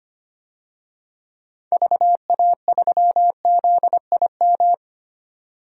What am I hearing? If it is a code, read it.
Morse code, heard as a keyed tone, about 25 wpm, 700 Hz. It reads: VA3ZIM
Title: DXCC-100 Call Signs - 6 Characters - 25wpm